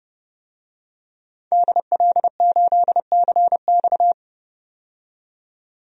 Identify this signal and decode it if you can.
Morse code, heard as a keyed tone, about 29 wpm, 700 Hz. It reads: DL8CX
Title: DXCC-100 Call Signs - 5 Characters - 30wpm